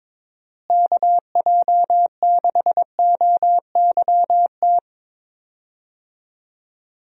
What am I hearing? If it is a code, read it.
Morse code, heard as a keyed tone, about 22 wpm, 700 Hz. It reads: KJ6OYT
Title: USA Call Signs - 6 Characters - Repeated 2x - 22wpm